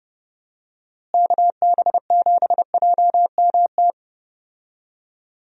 Morse code, heard as a keyed tone, about 31 wpm, 700 Hz. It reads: KB7JMT
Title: USA Call Signs - 6 Characters - Repeated 3x - 30wpm